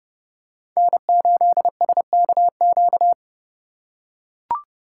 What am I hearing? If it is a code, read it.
Morse code, heard as a keyed tone, about 32 wpm, 700 Hz. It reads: N8SKQ
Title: USA Call Signs - 5 Characters - 3x Character Spacing - 30wpm